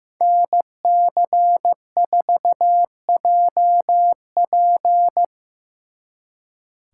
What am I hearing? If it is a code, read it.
Morse code, heard as a keyed tone, about 16 wpm, 700 Hz. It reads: NC4JP
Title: Speed Racing - USA 2x2 Call Signs 15wpm